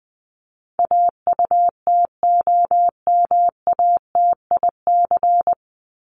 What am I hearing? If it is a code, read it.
Morse code, heard as a keyed tone, about 20 wpm, 700 Hz. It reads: AUTOMATIC
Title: Top 1751-2000 Words 20wpm